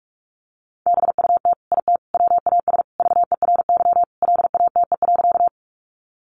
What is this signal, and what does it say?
Morse code, heard as a keyed tone, about 44 wpm, 700 Hz. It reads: BUT IT WAS VERY LATE.
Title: Sentences from Top 300 Words 45wpm